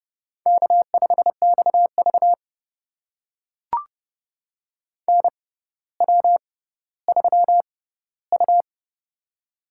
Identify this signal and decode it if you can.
Morse code, heard as a keyed tone, about 30 wpm, spaced as for 15 wpm, 700 Hz. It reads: K5XV NW3U
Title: USA Call Signs - 4 Characters - 6x Character Spacing - 30wpm